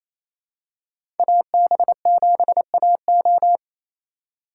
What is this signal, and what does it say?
Morse code, heard as a keyed tone, about 28 wpm, 700 Hz. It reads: AB7AO
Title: USA Call Signs - 3x Character Spacing - 28wpm